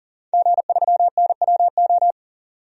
Morse code, heard as a keyed tone, about 40 wpm, 700 Hz. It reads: G3NWO
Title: DXCC-100 Call Signs - 5 Characters - 40wpm